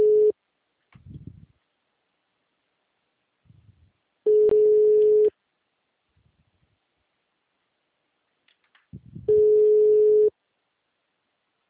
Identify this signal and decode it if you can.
Ringback tone (Europe, 425 Hz).